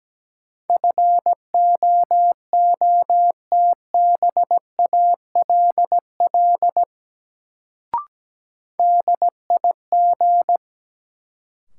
Morse code, heard as a keyed tone, about 17 wpm, 700 Hz. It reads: FOOTBALL DIG
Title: Sentences from Top 1600 Words - New Word Drill Alt - 17wpm